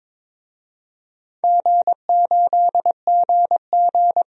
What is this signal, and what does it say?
Morse code, heard as a keyed tone, about 22 wpm, 700 Hz. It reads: G8GG
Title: DXCC-100 Call Signs - 4 Characters - 22wpm